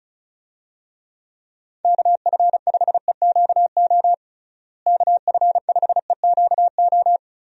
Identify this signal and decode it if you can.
Morse code, heard as a keyed tone, about 36 wpm, 700 Hz. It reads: KF5EQO KF5EQO
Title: USA Call Signs - 6 Characters - Repeated 2x - 35wpm